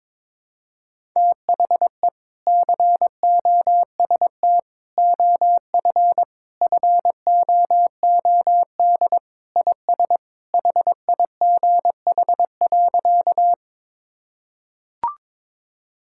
Morse code, heard as a keyed tone, about 22 wpm, 700 Hz. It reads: THE COST OF FOOD IS HIGH.
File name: practice / 22wpm / Sentences with 3-Word Phrases - 22wpm